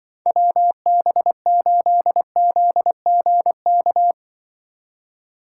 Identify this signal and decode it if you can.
Morse code, heard as a keyed tone, about 23 wpm, 700 Hz. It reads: WB8ZGK
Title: Speed Racing - USA 2x3 Call Signs 20wpm